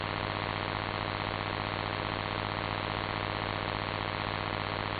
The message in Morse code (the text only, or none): none